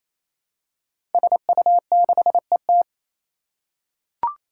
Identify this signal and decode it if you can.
Morse code, heard as a keyed tone, about 27 wpm, 700 Hz. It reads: SU6ET
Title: Five Challenging Letter-Number Combinations 28wpm